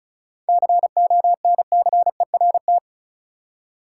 Morse code, heard as a keyed tone, about 36 wpm, 700 Hz. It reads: CONCERT
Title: Seven Letter Words - 35wpm